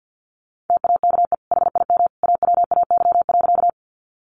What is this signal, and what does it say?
Morse code, heard as a keyed tone, about 50 wpm, 700 Hz. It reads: TAKE HIM AWAY.
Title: Sentences from Top 600 Words 50wpm